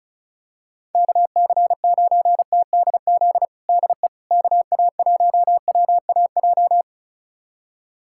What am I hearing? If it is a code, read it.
Morse code, heard as a keyed tone, about 36 wpm, 690 Hz. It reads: KC9TDZ DE KA1WAJ
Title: USA Callsign DE Callsign - 6 Characters - 35wpm